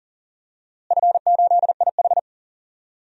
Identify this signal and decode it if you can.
Morse code, heard as a keyed tone, about 40 wpm, 700 Hz. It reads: F8IH
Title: DXCC-100 Call Signs - 4 Characters - Repeated 3x - 40wpm